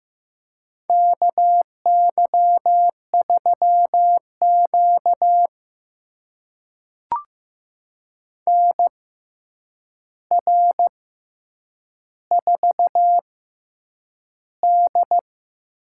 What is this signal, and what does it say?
Morse code, heard as a keyed tone, about 15 wpm, spaced as for 8 wpm, 700 Hz. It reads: KY3Q NR4D
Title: USA Call Signs - 4 Characters - 6x Character Spacing - 15wpm